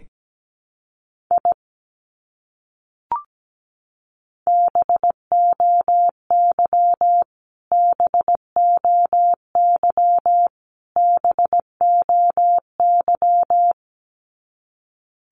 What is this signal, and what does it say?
Morse code, heard as a keyed tone, about 17 wpm, 700 Hz. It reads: I BOY BOY BOY
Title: Top 200 Words Repeated 17wpm